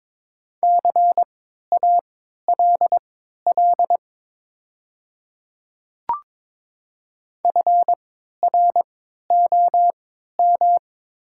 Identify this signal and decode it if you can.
Morse code, heard as a keyed tone, about 22 wpm, spaced as for 12 wpm, 700 Hz. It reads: CALL FROM
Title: 3x Character Spacing - Top 200 Words - 22wpm